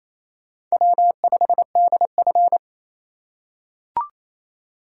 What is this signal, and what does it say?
Morse code, heard as a keyed tone, about 28 wpm, 700 Hz. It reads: W5DF
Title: USA Call Signs - 4 Characters - 4x Character Spacing - 28wpm